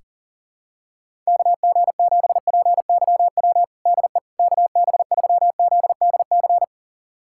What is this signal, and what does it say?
Morse code, heard as a keyed tone, about 37 wpm, 700 Hz. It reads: KG7PYW DE KB3ZDC
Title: USA Callsign DE Callsign - 6 Characters - 40wpm